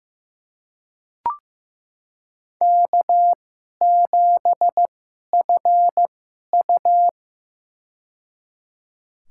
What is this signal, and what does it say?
Morse code, heard as a keyed tone, about 15 wpm, spaced as for 11 wpm, 700 Hz. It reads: K7FU
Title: USA Call Signs - 2x Character Spacing - 15wpm